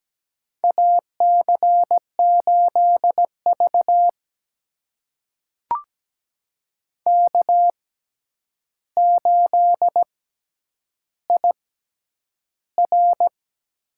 Morse code, heard as a keyed tone, about 17 wpm, spaced as for 9 wpm, 700 Hz. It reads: AC8V K8IR
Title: USA Call Signs - 4 Characters - 6x Character Spacing - 17wpm